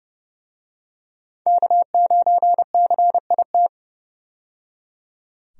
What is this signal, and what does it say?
Morse code, heard as a keyed tone, about 30 wpm, 700 Hz. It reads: K9CIT
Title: USA Call Signs - 5 Characters - 30wpm